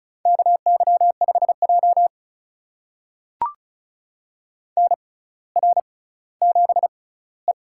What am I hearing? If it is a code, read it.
Morse code, heard as a keyed tone, about 36 wpm, spaced as for 18 wpm, 700 Hz. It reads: KY5J NR7E
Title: USA Call Signs - 4 Characters - 6x Character Spacing - 35wpm